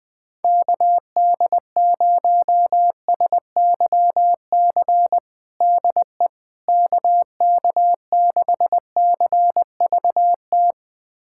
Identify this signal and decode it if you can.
Morse code, heard as a keyed tone, about 20 wpm, 690 Hz. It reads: KD0SYC DE KK6CVT
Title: USA Callsign DE Callsign 20wpm